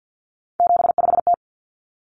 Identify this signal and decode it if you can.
Morse code, heard as a keyed tone, about 50 wpm, 700 Hz. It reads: 75T
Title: Common Cut Numbers - 3-Digit 50wpm